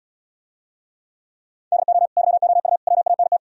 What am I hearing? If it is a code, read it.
Morse code, heard as a keyed tone, about 77 wpm, 700 Hz. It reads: DO YOU WANT